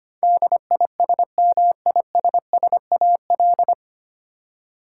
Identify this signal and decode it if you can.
Morse code, heard as a keyed tone, about 25 wpm, 700 Hz. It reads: DISMISSAL
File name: practice / 25wpm / 9 Letter Words - First 3 Letters Given - 25wpm